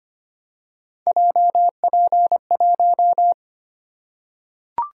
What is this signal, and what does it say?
Morse code, heard as a keyed tone, about 25 wpm, 700 Hz. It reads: JP1
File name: Three Letter-Number Combinations - 25wpm